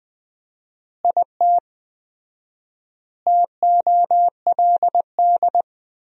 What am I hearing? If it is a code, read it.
Morse code, heard as a keyed tone, about 19 wpm, 700 Hz. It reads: IT TOLD
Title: 4x Word-Spacing - Sentences from Top 400 Words - 20wpm